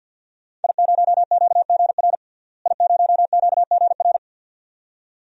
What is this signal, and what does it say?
Morse code, heard as a keyed tone, about 47 wpm, 700 Hz. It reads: I0QGR I0QGR
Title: DXCC-100 Call Signs - 5 Characters - Repeated 2x - 50wpm